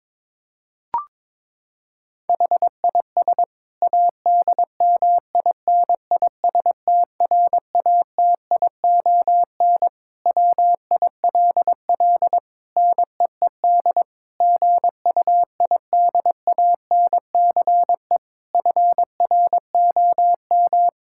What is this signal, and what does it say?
Morse code, heard as a keyed tone, about 22 wpm, 700 Hz. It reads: HIS ADMINISTRATION WILL NEED GUIDANCE FROM
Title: Sentences from Top 1600 Words 22wpm